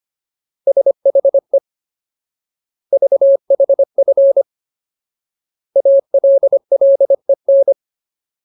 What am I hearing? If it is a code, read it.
Morse code, heard as a keyed tone, about 25 wpm, 550 Hz. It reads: SHE VHF ALLEN
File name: Ultimate Word List / 25WPM Ultimate